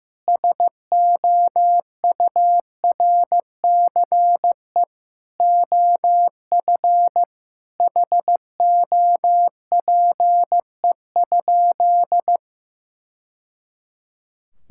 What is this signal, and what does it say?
Morse code, heard as a keyed tone, about 15 wpm, 700 Hz. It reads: SOURCE OF HOPE?
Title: Sentences from Top 1000 Words - Review Part 5 - 15wpm